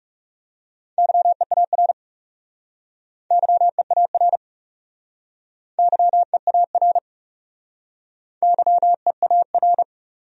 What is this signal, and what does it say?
Morse code, heard as a keyed tone, about 38 wpm, 700 Hz. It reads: YEAR YEAR YEAR YEAR